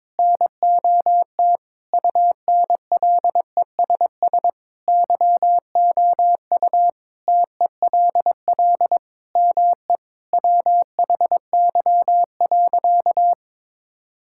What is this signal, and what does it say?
Morse code, heard as a keyed tone, about 22 wpm, 700 Hz. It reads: NOT UNLESS YOU TELL ME WHY.
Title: Sentences from Top 1000 Words - Review Part 6 - 22wpm